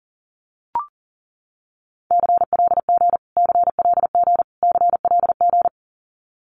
Morse code, heard as a keyed tone, about 40 wpm, 700 Hz. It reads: CLG CLG CLG
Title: Abbreviations Repeated 3x - 40wpm